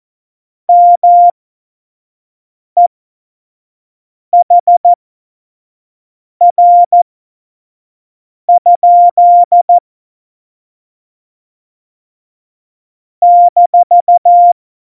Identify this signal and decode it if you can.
Morse code, heard as a keyed tone, about 14 wpm, spaced as for 5 wpm, 700 Hz. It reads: MEHR? -
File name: Morsekurs / Witze / Witz 07. November 2017, 25 F70